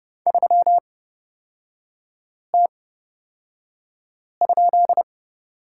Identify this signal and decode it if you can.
Morse code, heard as a keyed tone, about 31 wpm, spaced as for 5 wpm, 690 Hz. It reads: 3T?
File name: Lesson 37e - Letter x - Three Characters 30wpm